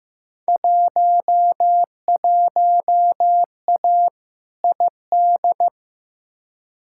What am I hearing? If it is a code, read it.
Morse code, heard as a keyed tone, about 15 wpm, 700 Hz. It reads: 11A ID
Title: Field Day Exchanges Encore - 15wpm